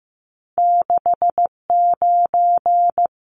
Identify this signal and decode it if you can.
Morse code, heard as a keyed tone, about 15 wpm, 700 Hz. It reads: 69
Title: Numbers - 2-Digit 15wpm